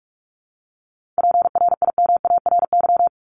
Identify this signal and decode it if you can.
Morse code, heard as a keyed tone, about 44 wpm, 700 Hz. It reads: PRIMARY
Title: LY Suffix - Top 100 Words with Base Words 45wpm